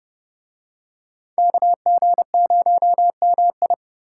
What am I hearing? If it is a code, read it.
Morse code, heard as a keyed tone, about 28 wpm, 700 Hz. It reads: KG0MI